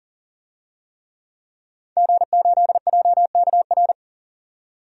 Morse code, heard as a keyed tone, about 40 wpm, 700 Hz. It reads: G8JKR
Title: DXCC-100 Call Signs - 5 Characters - 40wpm